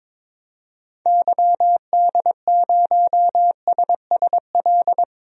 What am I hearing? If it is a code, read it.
Morse code, heard as a keyed tone, about 22 wpm, 700 Hz. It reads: YD0SSL
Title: DXCC-100 Call Signs - 6 Characters - Repeated 2x - 22wpm